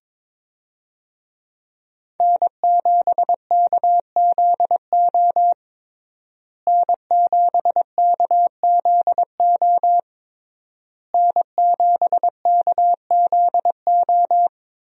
Morse code, heard as a keyed tone, about 22 wpm, 700 Hz. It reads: N7KZO N7KZO N7KZO
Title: USA Call Signs - 5 Characters - Repeated 3x - 22wpm